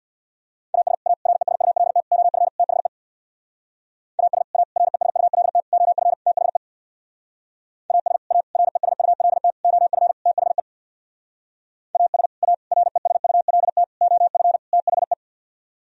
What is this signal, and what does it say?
Morse code, heard as a keyed tone, about 62 wpm, 700 Hz. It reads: AS A RESULT OF THE AS A RESULT OF THE AS A RESULT OF THE AS A RESULT OF THE
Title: Speed-Racing - Sets of 5 Words 50wpm